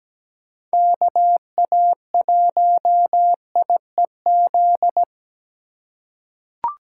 Morse code, heard as a keyed tone, about 17 wpm, 700 Hz. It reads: KA1IEZ
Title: USA Call Signs - 3x Character Spacing - 17wpm